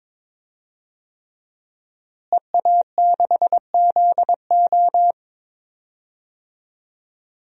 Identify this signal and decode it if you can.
Morse code, heard as a keyed tone, about 22 wpm, 700 Hz. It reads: EA6ZO
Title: DXCC-100 Call Signs - 5 Characters - Repeated 3x - 22wpm